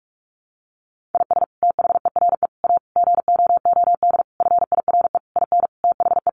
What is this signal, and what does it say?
Morse code, heard as a keyed tone, about 44 wpm, 700 Hz. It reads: IS THERE A GOOD FIRE IN THE
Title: Sentences from Top 1400 Words 45wpm